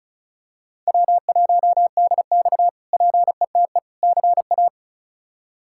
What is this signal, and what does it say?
Morse code, heard as a keyed tone, about 36 wpm, 700 Hz. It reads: W1DX PETE CA
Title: K1USN SST QSO Practice - 35wpm